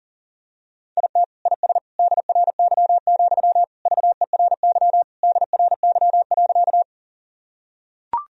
Morse code, heard as a keyed tone, about 40 wpm, 700 Hz. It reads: IT IS DRY, VERY DRY.
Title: Sentences from Top 1000 Words - Review Part 3 - 40wpm